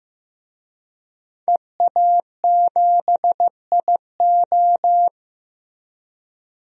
Morse code, heard as a keyed tone, about 15 wpm, 700 Hz. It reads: EA7IO
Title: DXCC-100 Call Signs - 5 Characters - Repeated 3x - 15wpm